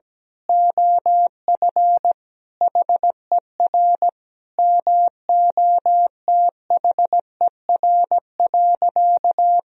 Morse code, heard as a keyed tone, about 17 wpm, 700 Hz. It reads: OF HER MOTHER.
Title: Sentences from Top 1000 Words - Review Part 4 - 17wpm